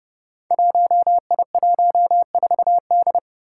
Morse code, heard as a keyed tone, about 30 wpm, 700 Hz. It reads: 1I14D
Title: Five Challenging Letter-Number Combinations 30wpm